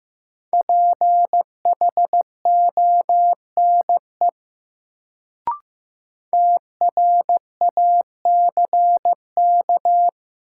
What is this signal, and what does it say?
Morse code, heard as a keyed tone, about 15 wpm, 700 Hz. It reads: PHONE TRACK
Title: Sentences from Top 1100 Words - New Word Drill Alt - 15wpm